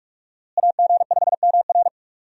Morse code, heard as a keyed tone, about 46 wpm, 700 Hz. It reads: AG5MR
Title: USA Call Signs - 5 Characters - Repeated 3x - 45wpm